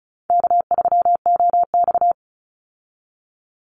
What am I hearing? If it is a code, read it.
Morse code, heard as a keyed tone, about 34 wpm, 700 Hz. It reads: K3OX
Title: Easy Call Signs Repeated 3x - 35wpm